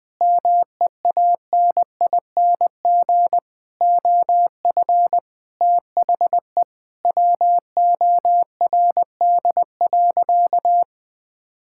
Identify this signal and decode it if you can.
Morse code, heard as a keyed tone, about 20 wpm, 700 Hz. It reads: MEANING OF THE WORD.
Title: Sentences from Top 1000 Words - Review Part 6 - 20wpm